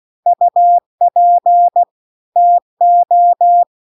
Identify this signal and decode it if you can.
Morse code, heard as a keyed tone, about 16 wpm, 700 Hz. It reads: UP TO